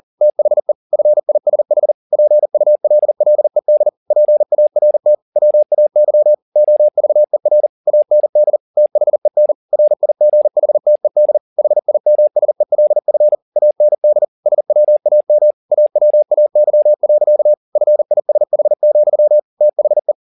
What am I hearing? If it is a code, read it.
Morse code, heard as a keyed tone, about 40 wpm, 600 Hz. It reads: THE FISH PULLED PART WAY OVER AND THEN RIGHTED HIMSELF AND SWAM AWAY. FISH, THE
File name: the old man and the sea 2 40wpm